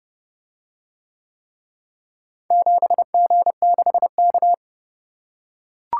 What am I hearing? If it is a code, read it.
Morse code, heard as a keyed tone, about 31 wpm, 700 Hz. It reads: 7G6K